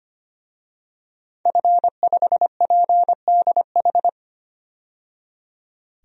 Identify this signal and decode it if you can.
Morse code, heard as a keyed tone, about 25 wpm, 700 Hz. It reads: F5PDH
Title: DXCC-100 Call Signs - 5 Characters - 25wpm